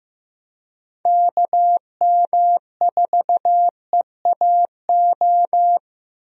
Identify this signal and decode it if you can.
Morse code, heard as a keyed tone, about 15 wpm, 700 Hz. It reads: KM4EAO
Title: USA Call Signs - 6 Characters - Repeated 3x - 15wpm